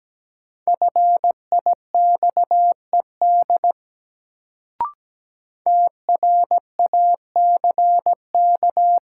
Morse code, heard as a keyed tone, about 17 wpm, 700 Hz. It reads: FIXED TRACK
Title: Sentences from Top 1100 Words - New Word Drill - 17wpm